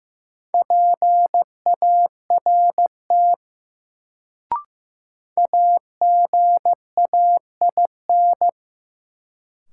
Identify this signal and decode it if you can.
Morse code, heard as a keyed tone, about 15 wpm, 700 Hz. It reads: PART AGAIN
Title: Sentences from Top 200 Words - New Word Drill - 15wpm